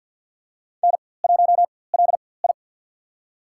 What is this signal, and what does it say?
Morse code, heard as a keyed tone, about 46 wpm, spaced as for 23 wpm, 700 Hz. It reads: N1LI
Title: USA Call Signs - 4x Character Spacing - 50wpm